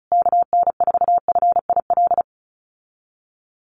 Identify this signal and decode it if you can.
Morse code, heard as a keyed tone, about 34 wpm, 700 Hz. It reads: KN4FIL
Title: USA Call Signs Repeated 3x 35wpm